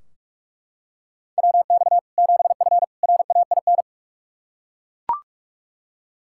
Morse code, heard as a keyed tone, about 41 wpm, 700 Hz. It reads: WX 7F RAIN
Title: Practice QSO - Exchange 2 - Info - 45wpm